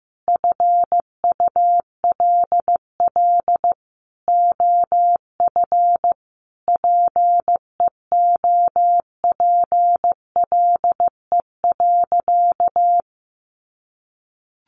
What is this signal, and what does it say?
Morse code, heard as a keyed tone, about 15 wpm, 700 Hz. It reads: FULL OF PEOPLE.
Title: Sentences from Top 500 Words 15wpm